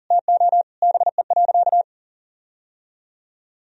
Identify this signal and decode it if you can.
Morse code, heard as a keyed tone, about 40 wpm, 700 Hz. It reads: TO BE.
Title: Sentences from Top 1000 Words - Review Part 3 - 40wpm